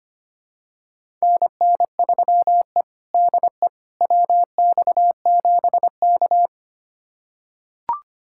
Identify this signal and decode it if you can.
Morse code, heard as a keyed tone, about 25 wpm, 700 Hz. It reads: NN3E DE WX7K
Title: USA Callsign DE Callsign - 4 Characters - 25wpm